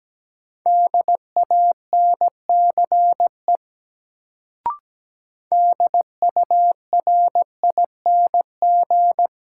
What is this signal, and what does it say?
Morse code, heard as a keyed tone, about 17 wpm, 700 Hz. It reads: DANCE DURING
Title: Sentences from Top 700 Words - New Word Drill - 17wpm